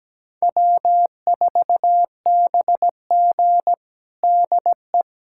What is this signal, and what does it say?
Morse code, heard as a keyed tone, about 17 wpm, 700 Hz. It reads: W4BG DE
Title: USA Callsign DE Callsign - 4 Characters - 17wpm